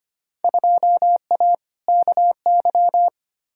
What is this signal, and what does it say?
Morse code, heard as a keyed tone, about 25 wpm, 700 Hz. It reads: 2A KY
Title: Field Day Exchanges - 25wpm